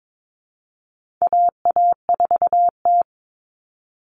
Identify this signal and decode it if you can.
Morse code, heard as a keyed tone, about 22 wpm, 700 Hz. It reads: AA4T
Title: All Cut Numbers - 4-Digit 22wpm